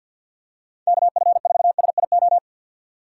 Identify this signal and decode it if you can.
Morse code, heard as a keyed tone, about 51 wpm, 700 Hz. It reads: KF4SIO
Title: USA Call Signs - 3x Character Spacing - 50wpm